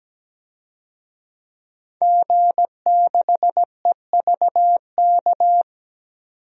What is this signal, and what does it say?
Morse code, heard as a keyed tone, about 17 wpm, 700 Hz. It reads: G6EVK
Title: DXCC-100 Call Signs - 5 Characters - 17wpm